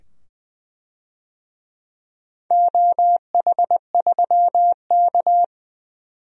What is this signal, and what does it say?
Morse code, heard as a keyed tone, about 20 wpm, 700 Hz. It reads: OH3K
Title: DXCC-100 Call Signs - 4 Characters - 20wpm